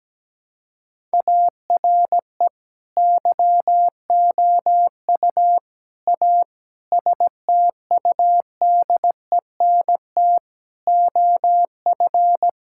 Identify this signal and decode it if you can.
Morse code, heard as a keyed tone, about 17 wpm, 700 Hz. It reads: ARE YOU A STUDENT OF
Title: Sentences from Top 1000 Words - Review Part 6 - 17wpm